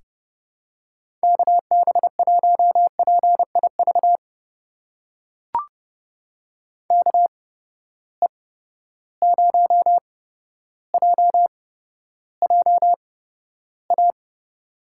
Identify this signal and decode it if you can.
Morse code, heard as a keyed tone, about 30 wpm, spaced as for 13 wpm, 700 Hz. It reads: KB1PIV KE0JJA
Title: USA Call Signs - 6 Characters - 8x Character Spacing - 30wpm